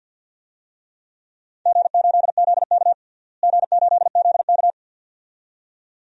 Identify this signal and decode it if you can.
Morse code, heard as a keyed tone, about 47 wpm, 700 Hz. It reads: G8ZK G8ZK
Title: DXCC-100 Call Signs - 4 Characters - Repeated 2x - 50wpm